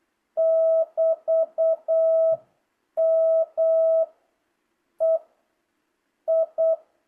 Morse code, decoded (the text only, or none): <BT> M E I